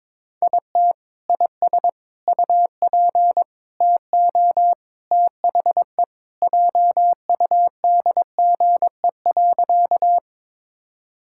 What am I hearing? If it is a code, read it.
Morse code, heard as a keyed tone, about 22 wpm, 700 Hz. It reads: IT IS UP TO THE JUDGE.
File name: practice / 22wpm / Sentences with 3-Word Phrases - 22wpm